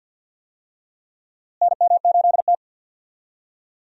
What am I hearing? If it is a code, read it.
Morse code, heard as a keyed tone, about 49 wpm, 700 Hz. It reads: NM8T
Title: USA Call Signs - 4 Characters - Repeated 3x - 50wpm